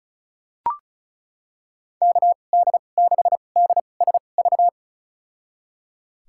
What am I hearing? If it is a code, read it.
Morse code, heard as a keyed tone, about 36 wpm, spaced as for 25 wpm, 700 Hz. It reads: KD6DSV
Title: USA Call Signs - 2x Character Spacing - 35wpm